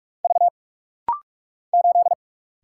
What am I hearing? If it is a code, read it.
Morse code, heard as a keyed tone, about 44 wpm, 700 Hz. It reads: V 8
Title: Single Letter-Number Alt - Rapid-Fire 45wpm